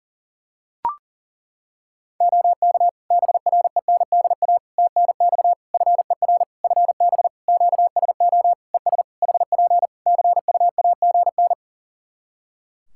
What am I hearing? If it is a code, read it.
Morse code, heard as a keyed tone, about 40 wpm, 700 Hz. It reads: OK BRENDA TNX FER FB QSO ES HP CUAGN
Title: Practice QSO - Exchange 4 - Ending - 40wpm